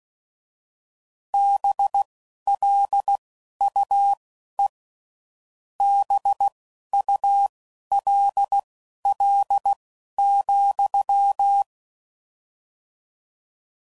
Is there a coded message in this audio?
Yes: Morse code from 1.3 to 11.6 s, heard as a keyed tone, about 16 wpm, spaced as for 11 wpm, 800 Hz. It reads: BLUE BULL,